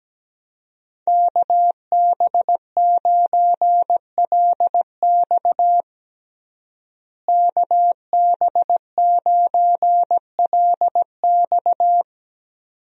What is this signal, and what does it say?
Morse code, heard as a keyed tone, about 17 wpm, 700 Hz. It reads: KB9LX KB9LX
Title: USA Call Signs - 5 Characters - Repeated 2x - 17wpm